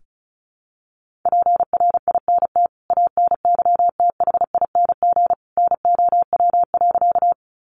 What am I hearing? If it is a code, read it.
Morse code, heard as a keyed tone, about 34 wpm, 700 Hz. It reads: PRINT ANYTHING NOW.
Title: Sentences from Top 1400 Words 35wpm